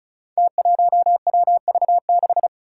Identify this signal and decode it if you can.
Morse code, heard as a keyed tone, about 36 wpm, 700 Hz. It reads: T1WV6
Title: Five Challenging Letter-Number Combinations 35wpm